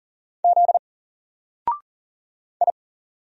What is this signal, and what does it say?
Morse code, heard as a keyed tone, about 39 wpm, 700 Hz. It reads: Z I
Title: Single Letter-Number Alt 40wpm